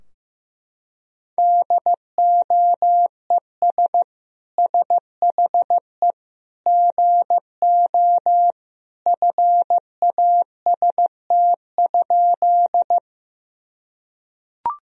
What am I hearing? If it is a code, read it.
Morse code, heard as a keyed tone, about 15 wpm, 700 Hz. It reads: DOES SHE GO FAST?
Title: Sentences from Top 1000 Words - Review Part 2 - 15wpm